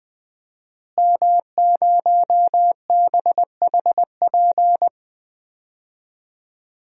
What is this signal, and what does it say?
Morse code, heard as a keyed tone, about 19 wpm, 700 Hz. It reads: M0BHP
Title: DXCC-100 Call Signs - 5 Characters - Repeated 3x - 20wpm